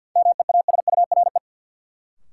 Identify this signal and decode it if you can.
Morse code, heard as a keyed tone, about 52 wpm, 700 Hz. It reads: MEASURE
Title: Sentences from Top 1200 Words - New Word Drill Alt - 50wpm